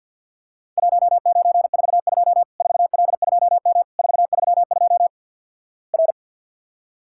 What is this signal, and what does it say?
Morse code, heard as a keyed tone, about 50 wpm, 700 Hz. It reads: 1942 4L2M 432 R
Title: Worked All Europe DX Contest - QTC Exchange with Repeats - 50wpm